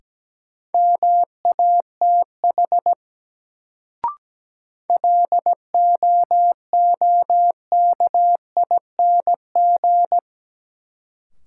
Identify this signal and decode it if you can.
Morse code, heard as a keyed tone, about 17 wpm, 700 Hz. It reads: MATH LOOKING
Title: Sentences from Top 1000 Words - Review Part 6 - New Word Drill - 17wpm